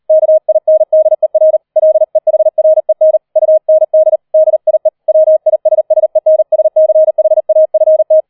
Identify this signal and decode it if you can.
Morse code, heard as a keyed tone, about 38 wpm, 630 Hz. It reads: KINDER LEHREN UND DIE WISSENSCHAFT
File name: 004 Vierter Streich 40wpm